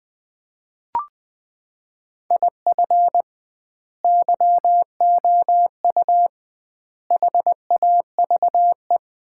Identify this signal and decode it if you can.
Morse code, heard as a keyed tone, about 20 wpm, 700 Hz. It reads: IF YOU HAVE
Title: 2x Word-Spacing - Sets of 4 Words - 20wpm